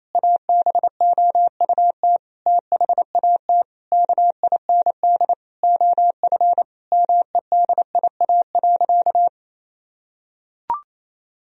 Morse code, heard as a keyed tone, about 28 wpm, 700 Hz. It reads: ABOUT THAT KIND OF MEDIA.
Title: Sentences from Top 1000 Words - Review Part 4 - 28wpm